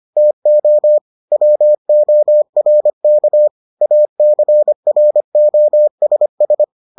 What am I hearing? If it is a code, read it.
Morse code, heard as a keyed tone, about 25 wpm, 600 Hz. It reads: TO WORK ACROSS